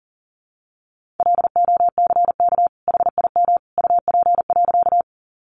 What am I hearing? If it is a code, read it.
Morse code, heard as a keyed tone, about 40 wpm, 700 Hz. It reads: LOCK HIM UP.
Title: Sentences from Top 1300 Words 40wpm